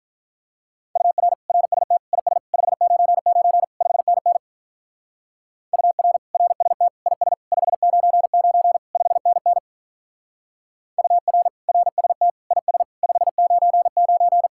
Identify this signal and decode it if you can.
Morse code, heard as a keyed tone, about 42 wpm, 700 Hz. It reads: UR RST IS 599 5NN UR RST IS 599 5NN UR RST IS 599
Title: Speed-Racing Common QSO Phrases Encore - 35wpm